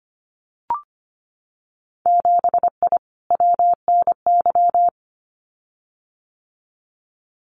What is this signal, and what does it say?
Morse code, heard as a keyed tone, about 25 wpm, 700 Hz. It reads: 7I WNY
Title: Winter Field Day Exchanges Encore 25wpm